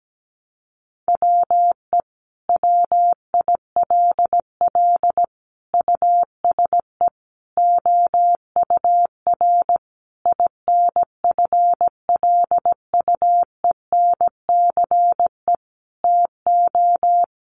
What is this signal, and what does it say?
Morse code, heard as a keyed tone, about 17 wpm, 700 Hz. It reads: WE WILL USE OUR INFLUENCE TO
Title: Sentences from Top 1100 Words 17wpm